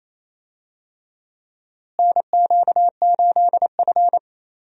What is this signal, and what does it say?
Morse code, heard as a keyed tone, about 28 wpm, 700 Hz. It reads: NQ8F
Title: USA Call Signs - 4 Characters - Repeated 3x - 28wpm